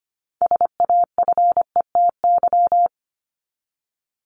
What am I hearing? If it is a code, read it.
Morse code, heard as a keyed tone, about 25 wpm, 700 Hz. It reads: SAFETY
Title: Top 501-750 Words 25wpm